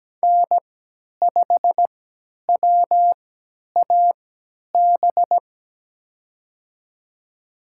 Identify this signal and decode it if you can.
Morse code, heard as a keyed tone, about 17 wpm, spaced as for 10 wpm, 700 Hz. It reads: N5WAB
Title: USA Call Signs - 5 Characters - 3x Character Spacing - 17wpm